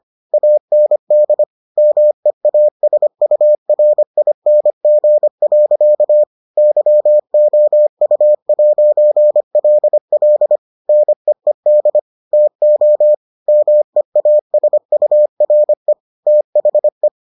Morse code, heard as a keyed tone, about 25 wpm, 600 Hz. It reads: AND MEASURING. YOU'LL NEED TO MEASURE THE